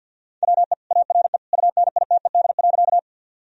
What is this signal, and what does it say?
Morse code, heard as a keyed tone, about 50 wpm, 700 Hz. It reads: WE ARE UNITED.
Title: Sentences from Top 1000 Words - Review Part 4 - 50wpm